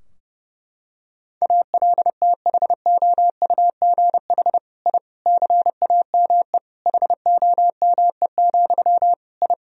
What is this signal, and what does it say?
Morse code, heard as a keyed tone, about 30 wpm, 700 Hz. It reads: ALTHOUGH I CAME HOME, I
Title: Sentences from Top 1000 Words - Review Part 3 - 30wpm